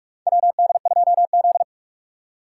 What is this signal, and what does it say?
Morse code, heard as a keyed tone, about 43 wpm, 700 Hz. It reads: WD2Z
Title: USA Call Signs - 4 Characters - 2x Character Spacing - 45wpm